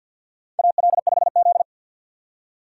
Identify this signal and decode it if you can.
Morse code, heard as a keyed tone, about 52 wpm, 700 Hz. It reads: AL5Z
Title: USA Call Signs - 4 Characters - 3x Character Spacing - 50wpm